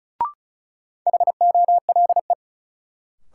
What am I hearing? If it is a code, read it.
Morse code, heard as a keyed tone, about 34 wpm, 700 Hz. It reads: HOLE
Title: Sentences from Top 1600 Words - New Word Drill - 35wpm